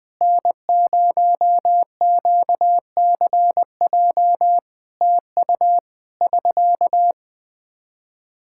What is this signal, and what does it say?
Morse code, heard as a keyed tone, about 20 wpm, 700 Hz. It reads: N0QCJ TU <SK>
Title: Practice QSO - Exchange 4 - Ending - 20wpm